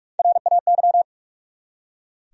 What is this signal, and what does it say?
Morse code, heard as a keyed tone, about 47 wpm, 700 Hz. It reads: RAY
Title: USA Common Names - Max 4 Letters - 45wpm